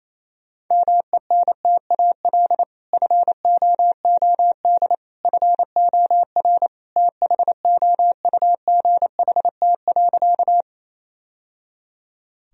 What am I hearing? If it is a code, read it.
Morse code, heard as a keyed tone, about 28 wpm, 700 Hz. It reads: MENTAL FOOD FOR THOUGHT.
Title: Sentences from Top 1000 Words - Review Part 4 - 28wpm